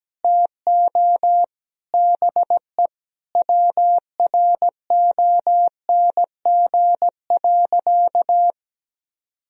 Morse code, heard as a keyed tone, about 17 wpm, 700 Hz. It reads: TO BE WRONG.
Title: Sentences from Top 1000 Words - Review Part 3 - 17wpm